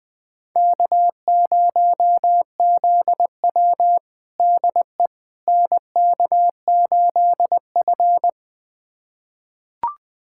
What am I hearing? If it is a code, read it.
Morse code, heard as a keyed tone, about 20 wpm, 700 Hz. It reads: K0ZW DE NK8F
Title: USA Callsign DE Callsign - 4 Characters - 20wpm